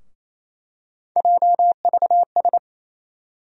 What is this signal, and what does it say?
Morse code, heard as a keyed tone, about 28 wpm, 700 Hz. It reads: JVS
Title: USA Call Sign Postfixes - 3 Letters 28wpm